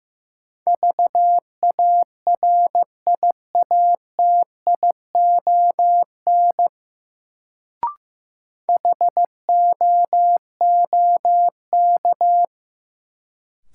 Morse code, heard as a keyed tone, about 15 wpm, 700 Hz. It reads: VARIATION HOOK
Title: Sentences from Top 1600 Words - New Word Drill - 15wpm